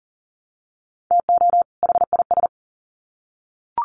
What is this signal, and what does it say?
Morse code, heard as a keyed tone, about 40 wpm, 700 Hz. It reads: TO HIS
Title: Sets of 2 Words 40wpm